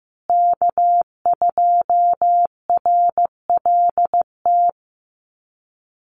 Morse code, heard as a keyed tone, about 15 wpm, 700 Hz. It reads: K2RLT
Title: USA Call Signs Repeated 2x 15wpm